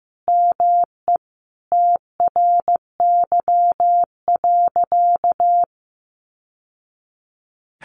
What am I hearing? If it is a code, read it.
Morse code, heard as a keyed tone, about 15 wpm, 700 Hz. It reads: ME TRY.
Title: Sentences from Top 600 Words 15wpm